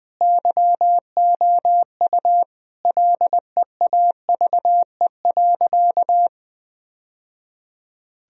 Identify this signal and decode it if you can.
Morse code, heard as a keyed tone, about 20 wpm, 700 Hz. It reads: YOU LEAVE.
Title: Sentences from Top 1000 Words - Review Part 4 - 20wpm